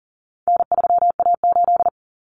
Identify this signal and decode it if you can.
Morse code, heard as a keyed tone, about 40 wpm, 700 Hz. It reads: N3A8
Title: Common Cut Numbers - 4-Digit 40wpm